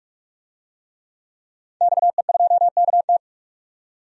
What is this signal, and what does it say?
Morse code, heard as a keyed tone, about 45 wpm, 700 Hz. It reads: XE2KT
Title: DXCC-100 Call Signs - 5 Characters - 45wpm